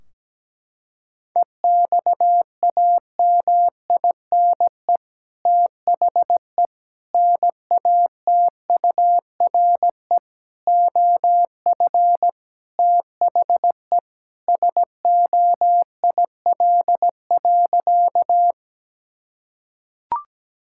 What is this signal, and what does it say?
Morse code, heard as a keyed tone, about 17 wpm, 700 Hz. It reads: EXAMINE THE NATURE OF THE SOIL.
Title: Sentences from Top 1600 Words 17wpm